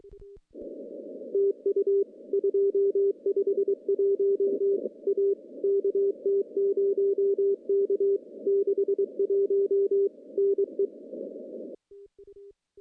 Morse code, heard as a keyed tone, about 22 wpm, 410 Hz. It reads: TU 251 A KT0K 61 NE
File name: contests / CW-SS-k5tr- -0300